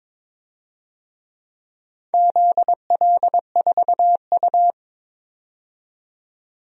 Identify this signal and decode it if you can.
Morse code, heard as a keyed tone, about 22 wpm, 700 Hz. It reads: ZL4U
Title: DXCC-100 Call Signs - 4 Characters - 22wpm